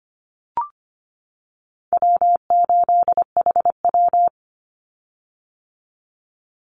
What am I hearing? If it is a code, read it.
Morse code, heard as a keyed tone, about 25 wpm, 700 Hz. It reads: W8HW
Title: Call Signs - 25wpm